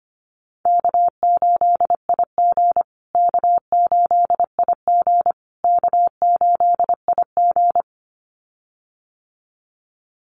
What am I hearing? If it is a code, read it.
Morse code, heard as a keyed tone, about 25 wpm, 700 Hz. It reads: K8IG K8IG K8IG
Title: USA Call Signs Repeated 3x 25wpm